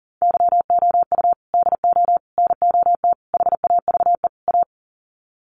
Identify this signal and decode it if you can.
Morse code, heard as a keyed tone, about 40 wpm, 700 Hz. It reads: YOU DO NOT HAVE A
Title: Sets of 5 Words 40wpm